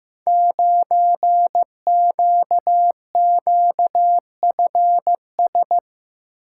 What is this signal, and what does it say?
Morse code, heard as a keyed tone, about 15 wpm, 700 Hz. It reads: 9QQFS
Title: Code Groups - 5 Letter:Numbers - 15wpm